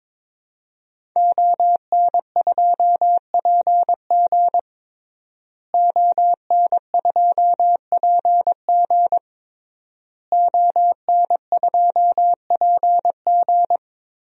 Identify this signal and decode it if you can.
Morse code, heard as a keyed tone, about 22 wpm, 700 Hz. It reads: ON2PG ON2PG ON2PG
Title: DXCC-100 Call Signs - 5 Characters - Repeated 3x - 22wpm